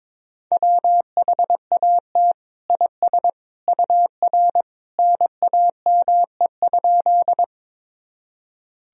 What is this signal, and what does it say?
Morse code, heard as a keyed tone, about 22 wpm, 700 Hz. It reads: WHAT IS UR NAME?
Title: Sentences from Top 200 Words and Abbreviations 22wpm